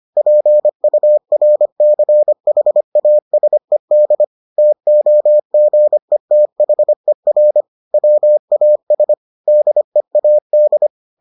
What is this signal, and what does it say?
Morse code, heard as a keyed tone, about 25 wpm, 600 Hz. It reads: PURCHASED TOGETHER WAS DEAD